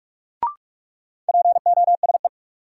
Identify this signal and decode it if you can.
Morse code, heard as a keyed tone, about 46 wpm, 700 Hz. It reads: POSE
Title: Sentences from Top 1000 Words - Review Part 2 - New Word Drill - 45wpm